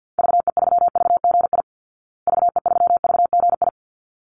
Morse code, heard as a keyed tone, about 48 wpm, 700 Hz. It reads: VE3VGI VE3VGI
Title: Call Signs Repeated 3x - 50wpm